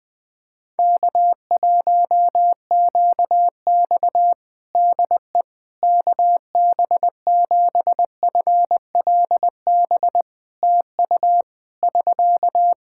Morse code, heard as a keyed tone, about 20 wpm, 700 Hz. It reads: K1QX DE KB7FLB TU <SK>
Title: Practice QSO - Exchange 4 - Ending - 20wpm